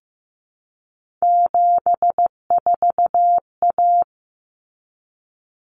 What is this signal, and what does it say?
Morse code, heard as a keyed tone, about 15 wpm, 700 Hz. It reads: 74A
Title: Common Cut Numbers - 3-Digit 15wpm